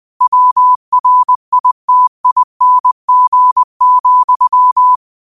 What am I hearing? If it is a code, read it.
Morse code, heard as a keyed tone, about 20 wpm, 1000 Hz. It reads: WRITING,
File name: qotdfiles / 20wpm